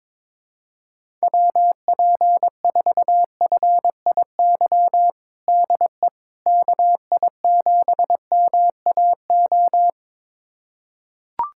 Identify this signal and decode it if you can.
Morse code, heard as a keyed tone, about 22 wpm, 700 Hz. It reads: WP4FIY DE KI7MAO
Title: USA Callsign DE Callsign 22wpm